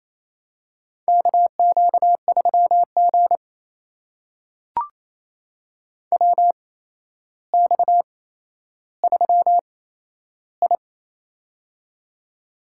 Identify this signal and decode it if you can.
Morse code, heard as a keyed tone, about 27 wpm, spaced as for 12 wpm, 700 Hz. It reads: KQ3G WX3I
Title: USA Call Signs - 4 Characters - 8x Character Spacing - 28wpm